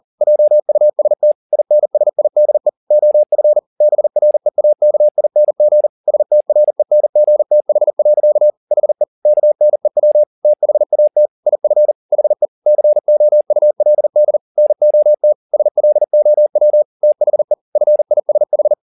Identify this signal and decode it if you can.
Morse code, heard as a keyed tone, about 40 wpm, 600 Hz. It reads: JUST INSIDE OF BREAKING STRENGTH. HE KNEW THAT IF HE COULD NOT SLOW THE FISH